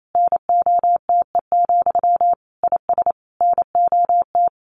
Morse code, heard as a keyed tone, about 28 wpm, 700 Hz. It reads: NOTE, IS NOT
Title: A Sound Of Thunder 28wpm